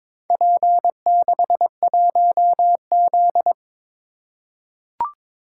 Morse code, heard as a keyed tone, about 22 wpm, 700 Hz. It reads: P61Z
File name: practice / 22wpm / Four Letter-Number Combinations - 22wpm